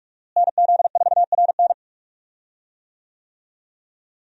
Morse code, heard as a keyed tone, about 42 wpm, 700 Hz. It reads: NZ4RN
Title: USA Call Signs - 5 Characters - Repeated 2x - 45wpm